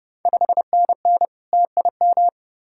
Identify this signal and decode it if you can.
Morse code, heard as a keyed tone, about 30 wpm, 690 Hz. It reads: 5NN TIM
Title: Ontario QSO Party Exchange 30wpm